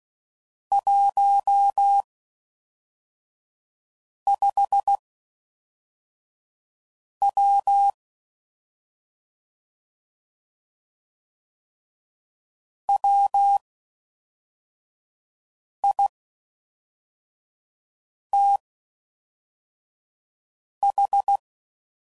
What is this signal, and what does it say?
Morse code, heard as a keyed tone, about 16 wpm, spaced as for 4 wpm, 800 Hz. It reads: 15W WITH